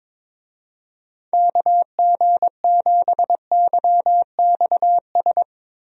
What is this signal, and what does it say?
Morse code, heard as a keyed tone, about 22 wpm, 700 Hz. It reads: KG7YXS
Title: USA Call Signs - 6 Characters - Repeated 3x - 22wpm